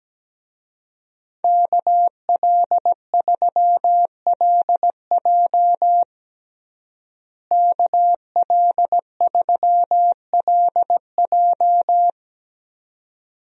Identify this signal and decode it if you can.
Morse code, heard as a keyed tone, about 17 wpm, 700 Hz. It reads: KL3LJ KL3LJ
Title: USA Call Signs - 5 Characters - Repeated 2x - 17wpm